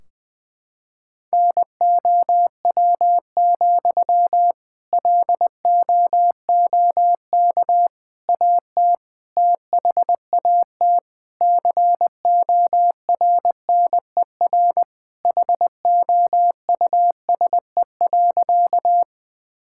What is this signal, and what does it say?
Morse code, heard as a keyed tone, about 20 wpm, 700 Hz. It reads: NOW, LOOK AT THAT CORNER HOUSE.
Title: Sentences from Top 1600 Words 20wpm